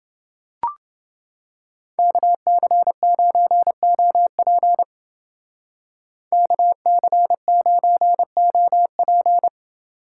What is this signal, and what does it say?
Morse code, heard as a keyed tone, about 29 wpm, 700 Hz. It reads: KC9OP KC9OP